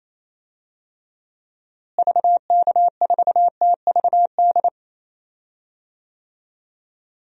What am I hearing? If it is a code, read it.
Morse code, heard as a keyed tone, about 28 wpm, 700 Hz. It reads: VK4TVD
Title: DXCC-100 Call Signs - 6 Characters - Repeated 2x - 28wpm